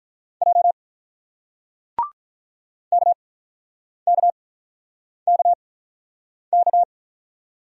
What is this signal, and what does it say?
Morse code, heard as a keyed tone, about 48 wpm, 700 Hz. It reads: J K K K K